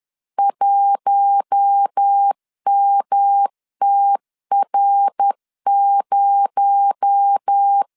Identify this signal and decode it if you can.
Morse code, heard as a keyed tone, about 10 wpm, 800 Hz. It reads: 1MTR0
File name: LEZIONI / 6 PKX0 / 6 PKX0 10wpm 2707